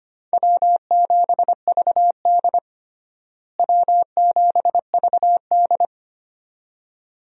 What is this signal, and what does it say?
Morse code, heard as a keyed tone, about 25 wpm, 700 Hz. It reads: W7VD W7VD
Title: USA Call Signs - 4 Characters - Repeated 2x - 25wpm